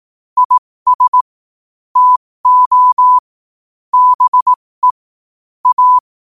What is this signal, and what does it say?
Morse code, heard as a keyed tone, about 18 wpm, spaced as for 15 wpm, 1000 Hz. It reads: IS TO BE A
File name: qotdfiles / 15wpm